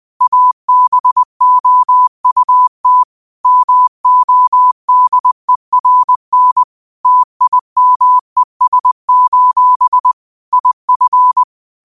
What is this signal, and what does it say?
Morse code, heard as a keyed tone, about 20 wpm, 1000 Hz. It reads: ABOUT MODERN TIMES: IF
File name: qotdfiles / 20wpm